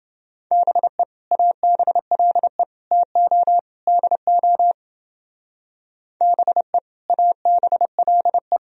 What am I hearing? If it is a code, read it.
Morse code, heard as a keyed tone, about 30 wpm, 700 Hz. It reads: BE ABLE TO DO BE ABLE